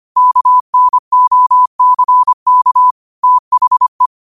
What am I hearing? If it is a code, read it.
Morse code, heard as a keyed tone, about 25 wpm, 1000 Hz. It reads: KNOCK THE
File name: qotdfiles / 25wpm